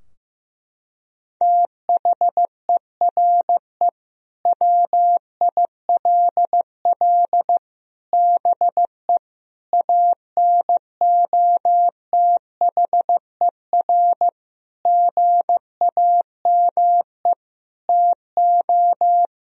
Morse code, heard as a keyed tone, about 15 wpm, 700 Hz. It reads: THERE WILL BE ANOTHER GAME TO
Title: Sentences from Top 1000 Words - Review Part 3 - 15wpm